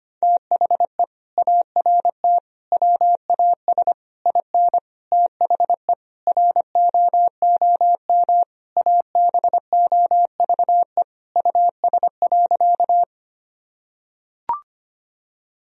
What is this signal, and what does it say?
Morse code, heard as a keyed tone, about 25 wpm, 700 Hz. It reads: THE ART WAS IN THE ROOM ABOVE US.
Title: Sentences from Top 1000 Words - Review Part 3 - 25wpm